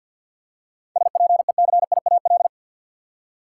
Morse code, heard as a keyed tone, about 51 wpm, 700 Hz. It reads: SPECIAL
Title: Seven Letter Words - 50wpm